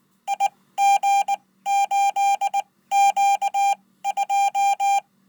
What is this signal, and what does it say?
Morse code, heard as a keyed tone, about 19 wpm, spaced as for 15 wpm, 800 Hz. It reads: IG8Q2
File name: Code